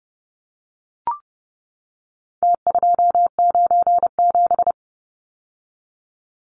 Morse code, heard as a keyed tone, about 30 wpm, 700 Hz. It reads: T297
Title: Numbers - Contest Style - 4-Digit 30wpm